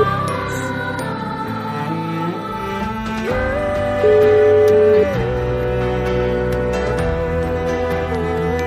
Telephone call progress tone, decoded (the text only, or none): ringback tone